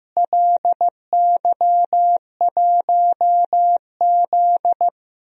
Morse code, heard as a keyed tone, about 16 wpm, 700 Hz. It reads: LY1Z
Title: DXCC-100 Call Signs - 4 Characters - Repeated 3x - 15wpm